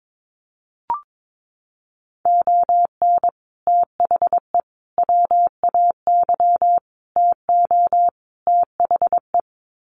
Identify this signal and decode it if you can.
Morse code, heard as a keyed tone, about 22 wpm, 700 Hz. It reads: ON THE WAY TO THE
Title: Sets of 5 Words 22wpm